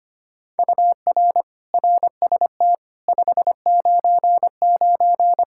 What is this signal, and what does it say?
Morse code, heard as a keyed tone, about 25 wpm, 700 Hz. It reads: UR RST 599
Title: K1USN SST QSO Practice 25wpm